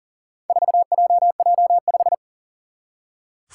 Morse code, heard as a keyed tone, about 40 wpm, 700 Hz. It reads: 4JJ5
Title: Four Challenging Letter-Number Combinations 40wpm